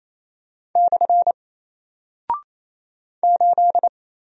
Morse code, heard as a keyed tone, about 28 wpm, 700 Hz. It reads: / 8
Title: Single Character (Letter:Number:Punctuation:Prosign) 28wpm